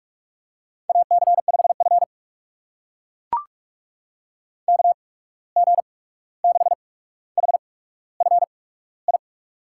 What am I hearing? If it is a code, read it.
Morse code, heard as a keyed tone, about 44 wpm, spaced as for 17 wpm, 690 Hz. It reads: AC5F KG6HFI
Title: USA Call Signs - 8x Character Spacing - 45wpm